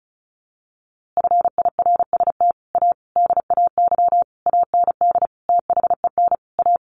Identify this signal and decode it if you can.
Morse code, heard as a keyed tone, about 34 wpm, 700 Hz. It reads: FIRST A DAY AND THEN A
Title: A Sound Of Thunder 35wpm